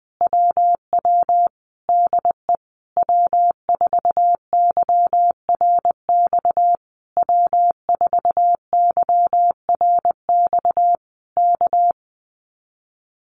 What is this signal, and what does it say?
Morse code, heard as a keyed tone, about 20 wpm, 700 Hz. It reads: WW DE W4YRX W4YRX K
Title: Common QSO Phrases Encore 20wpm